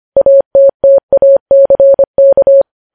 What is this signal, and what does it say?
Morse code, heard as a keyed tone, about 25 wpm, 550 Hz. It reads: ATTACK
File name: podcast / News headlines-1x-25x25